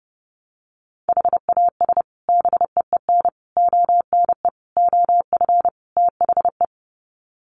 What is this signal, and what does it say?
Morse code, heard as a keyed tone, about 30 wpm, 700 Hz. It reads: HAS BEEN ONE OF THE
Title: Sets of 5 Words Encore 30wpm